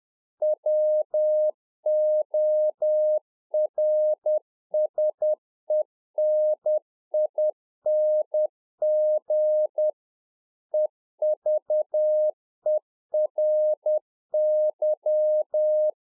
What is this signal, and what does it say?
Morse code, heard as a keyed tone, about 10 wpm, 610 Hz. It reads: WORSENING EVERY